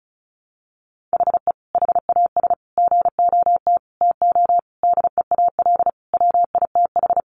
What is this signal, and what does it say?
Morse code, heard as a keyed tone, about 34 wpm, 700 Hz. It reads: HE HAS GOT TO DEAL WITH
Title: Sentences from Top 800 Words 35wpm